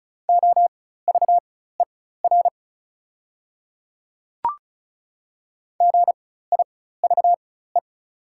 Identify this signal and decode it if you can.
Morse code, heard as a keyed tone, about 34 wpm, spaced as for 16 wpm, 690 Hz. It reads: OVER GIVE
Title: 4x Character Spacing - Top 200 Words - 35wpm